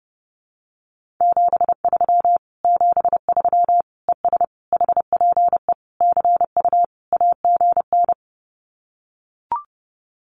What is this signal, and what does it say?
Morse code, heard as a keyed tone, about 30 wpm, 700 Hz. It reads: 73 73 ES HPE CU AGN
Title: Common QSO Phrases Encore 30wpm